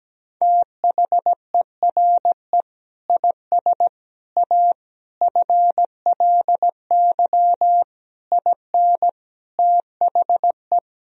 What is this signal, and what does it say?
Morse code, heard as a keyed tone, about 17 wpm, 700 Hz. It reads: THERE IS A FLY IN THE
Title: Sentences from Top 1000 Words - Review Part 4 - 17wpm